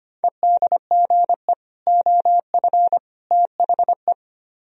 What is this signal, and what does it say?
Morse code, heard as a keyed tone, about 32 wpm, 700 Hz. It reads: EDGE OF THE